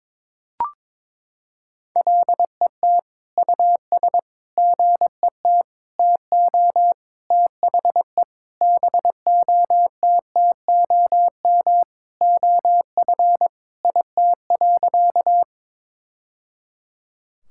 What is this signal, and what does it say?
Morse code, heard as a keyed tone, about 21 wpm, 700 Hz. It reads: LET US GET TO THE BOTTOM OF IT.
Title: Sentences from Top 1000 Words - Review Part 6 - 22wpm